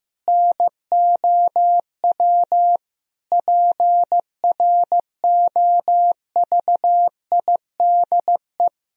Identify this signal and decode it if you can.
Morse code, heard as a keyed tone, about 15 wpm, 700 Hz. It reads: NOW PROVIDE
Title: Sentences from Top 1000 Words - Review Part 3 - 15wpm